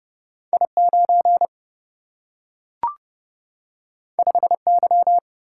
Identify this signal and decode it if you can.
Morse code, heard as a keyed tone, about 30 wpm, 700 Hz. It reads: I9 5Y
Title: Single-Letter Single-Number 30wpm